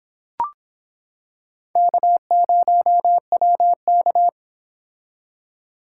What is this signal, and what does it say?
Morse code, heard as a keyed tone, about 25 wpm, 700 Hz. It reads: K0WK